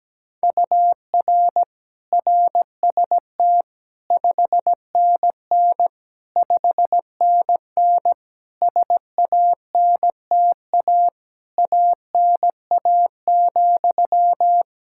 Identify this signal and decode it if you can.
Morse code, heard as a keyed tone, about 17 wpm, 700 Hz. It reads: UR RST 5NN 5NN SANTA ANA,
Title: Practice QSO - Exchange 1 - Beginning - 17wpm